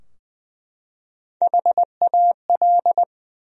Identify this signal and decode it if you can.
Morse code, heard as a keyed tone, about 20 wpm, 700 Hz. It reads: HAL
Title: USA Common Names - Max 5 Letters - 20wpm